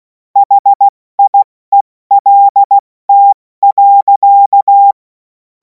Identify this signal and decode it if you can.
Morse code, heard as a keyed tone, about 16 wpm, 800 Hz. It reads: HIELT.
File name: Morsekurs / Witze / Witz 03. Februar 2026 70 F80